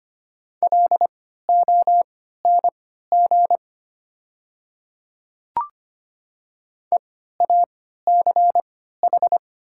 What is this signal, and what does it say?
Morse code, heard as a keyed tone, about 25 wpm, spaced as for 14 wpm, 690 Hz. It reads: LONG EACH
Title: 3x Character Spacing - Top 100 Words - 25wpm